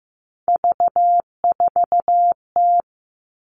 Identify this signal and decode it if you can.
Morse code, heard as a keyed tone, about 15 wpm, 700 Hz. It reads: V4T
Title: All Cut Numbers - 3-Digit 15wpm